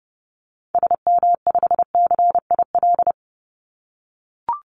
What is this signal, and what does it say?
Morse code, heard as a keyed tone, about 30 wpm, 700 Hz. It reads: SM5CIL
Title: Call Signs - 30wpm